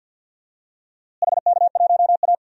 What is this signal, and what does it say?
Morse code, heard as a keyed tone, about 50 wpm, 700 Hz. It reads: HK1A
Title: DXCC-100 Call Signs - 4 Characters - Repeated 2x - 50wpm